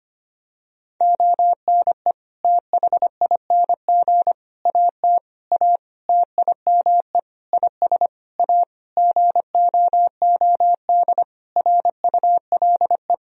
Morse code, heard as a keyed tone, about 25 wpm, 700 Hz. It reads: ONE THING AT A TIME IS A GOOD RULE
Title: Sentences from Top 1000 Words - Review Part 2 - 25wpm